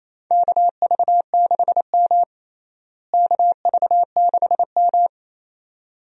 Morse code, heard as a keyed tone, about 27 wpm, 700 Hz. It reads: KV6M KV6M
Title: USA Call Signs - 4 Characters - Repeated 2x - 28wpm